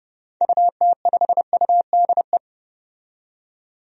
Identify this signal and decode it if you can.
Morse code, heard as a keyed tone, about 30 wpm, 700 Hz. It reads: UT5UDE
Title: DXCC-100 Call Signs - 6 Characters - 30wpm